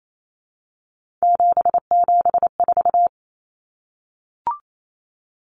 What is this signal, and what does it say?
Morse code, heard as a keyed tone, about 28 wpm, 700 Hz. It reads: 774
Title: Common Cut Numbers - 3-Digit 28wpm